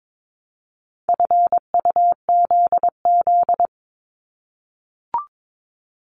Morse code, heard as a keyed tone, about 22 wpm, 700 Hz. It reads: FUZZ
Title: Multi-letter Phonemes 22wpm